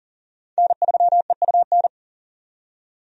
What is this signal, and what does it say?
Morse code, heard as a keyed tone, about 40 wpm, 700 Hz. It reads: N3EUN
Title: USA Call Signs - 5 Characters - 40wpm